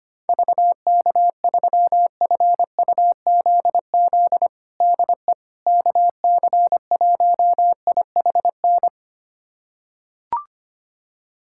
Morse code, heard as a keyed tone, about 25 wpm, 700 Hz. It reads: VK3FUZZ DE KC1IHN
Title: USA Callsign DE Callsign 25wpm